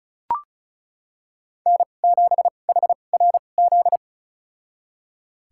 Morse code, heard as a keyed tone, about 36 wpm, spaced as for 25 wpm, 700 Hz. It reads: N7HRZ
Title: USA Call Signs - 2x Character Spacing - 35wpm